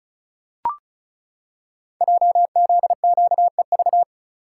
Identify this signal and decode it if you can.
Morse code, heard as a keyed tone, about 36 wpm, 690 Hz. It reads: JZQEV
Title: Code Groups - 5 Letters - 35wpm